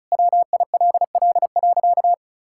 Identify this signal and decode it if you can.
Morse code, heard as a keyed tone, about 34 wpm, 700 Hz. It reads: WILL.
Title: Sentences from Top 1600 Words 35wpm